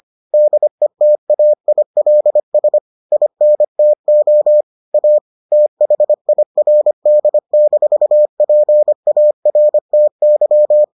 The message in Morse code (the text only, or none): DETAILS INTO A THIRD-PARTY